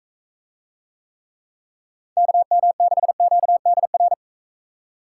Morse code, heard as a keyed tone, about 43 wpm, 700 Hz. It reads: KM6QDR